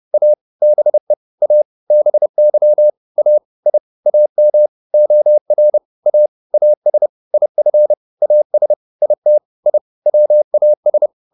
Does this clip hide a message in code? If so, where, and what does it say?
Yes: Morse code from 0.1 to 11.1 s, heard as a keyed tone, about 30 wpm, 600 Hz. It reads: A BE A BY A I AM OR A AS IF AS IT I WAS